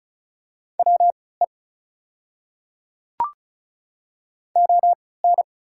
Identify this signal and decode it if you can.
Morse code, heard as a keyed tone, about 35 wpm, spaced as for 20 wpm, 690 Hz. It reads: WE ON
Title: 3x Character Spacing - Top 100 Words - 35wpm